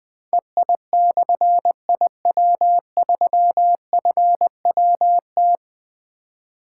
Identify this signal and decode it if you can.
Morse code, heard as a keyed tone, about 20 wpm, 700 Hz. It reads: EI/IW3FWT
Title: Difficult Call Signs 20wpm